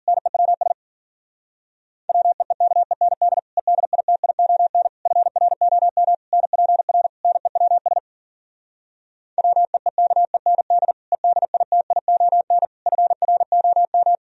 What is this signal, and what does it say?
Morse code, heard as a keyed tone, about 52 wpm, 700 Hz. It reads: NEWS WEEKEND EDITION FROM NPR NEWS WEEKEND EDITION FROM